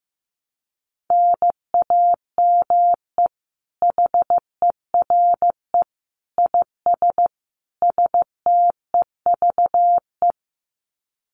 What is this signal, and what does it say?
Morse code, heard as a keyed tone, about 15 wpm, 700 Hz. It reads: NAME HERE IS STEVE
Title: Common QSO Phrases 15wpm